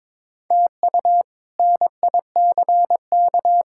Morse code, heard as a keyed tone, about 22 wpm, 700 Hz. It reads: TU NICK
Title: K1USN SST QSO Practice - 22wpm